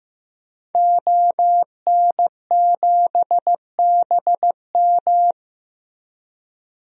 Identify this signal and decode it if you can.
Morse code, heard as a keyed tone, about 15 wpm, 700 Hz. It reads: ON7BM
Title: DXCC-100 Call Signs - 5 Characters - Repeated 3x - 15wpm